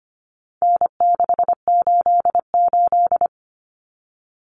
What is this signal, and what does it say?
Morse code, heard as a keyed tone, about 25 wpm, 700 Hz. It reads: N688
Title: Common Cut Numbers - 4-Digit 25wpm